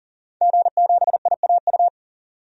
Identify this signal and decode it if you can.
Morse code, heard as a keyed tone, about 40 wpm, 700 Hz. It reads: G7IAU
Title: DXCC-100 Call Signs - 5 Characters - 40wpm